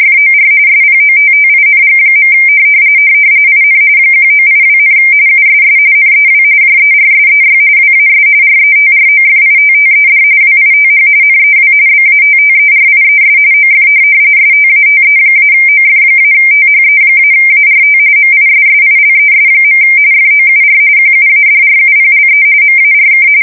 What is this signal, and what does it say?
RTTY, about 45.45 baud. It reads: FOR SPOTTING. IN 1982, USS NEW JERSEY WAS RETROFITTED WITH NEW BATTERIES OF STATE-OF-THE-ART MISSILES. FOUR QUADRUPLE MARK
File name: fsk170 / USS-New-Jersey